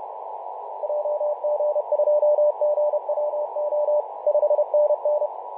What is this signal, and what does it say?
Morse code, heard as a keyed tone, about 31 wpm, 600 Hz, with radio noise. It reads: JG2GWO 5NN
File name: Mar/20 2349z FW5JJ 28.011MHz CW